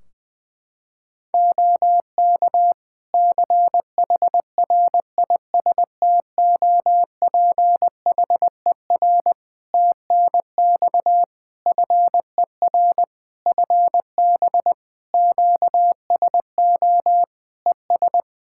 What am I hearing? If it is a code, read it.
Morse code, heard as a keyed tone, about 20 wpm, 700 Hz. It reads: OK CHRISTOPHER TNX FER FB QSO ES
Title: Practice QSO - Exchange 4 - Ending - 20wpm